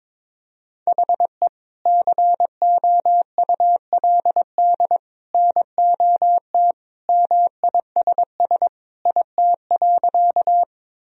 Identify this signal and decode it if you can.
Morse code, heard as a keyed tone, about 22 wpm, 700 Hz. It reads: HE COULD NOT MISS IT.
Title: Sentences from Top 1000 Words - Review Part 3 - 22wpm